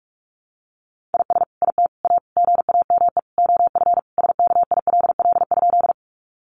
Morse code, heard as a keyed tone, about 44 wpm, 700 Hz. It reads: IS IT A GAME OF SKILL?
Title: Sentences from Top 1200 Words 45wpm